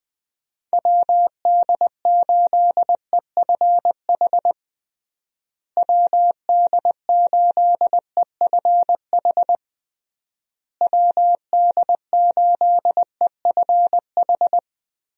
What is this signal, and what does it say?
Morse code, heard as a keyed tone, about 20 wpm, 700 Hz. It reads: WD8EFH WD8EFH WD8EFH
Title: USA Call Signs - 6 Characters - Repeated 3x - 20wpm